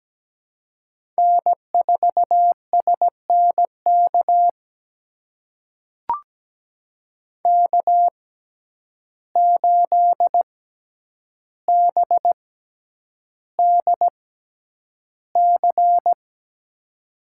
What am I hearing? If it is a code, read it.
Morse code, heard as a keyed tone, about 17 wpm, spaced as for 9 wpm, 700 Hz. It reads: N4SNK K8BDC
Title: USA Call Signs - 5 Characters - 6x Character Spacing - 17wpm